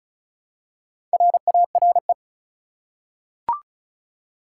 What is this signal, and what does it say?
Morse code, heard as a keyed tone, about 36 wpm, 700 Hz. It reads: RARE
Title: Four Letter Words - 35wpm